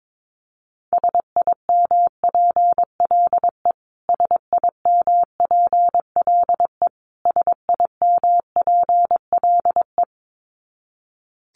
Morse code, heard as a keyed tone, about 22 wpm, 700 Hz. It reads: SIMPLE SIMPLE SIMPLE
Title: Top 500 Words Repeated 22wpm